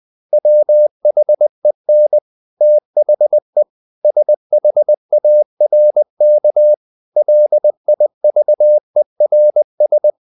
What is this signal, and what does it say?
Morse code, heard as a keyed tone, about 20 wpm, 600 Hz. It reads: WHEN THE SHARK LIVERS
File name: the old man and the sea / the old man and the sea part 1 20wpm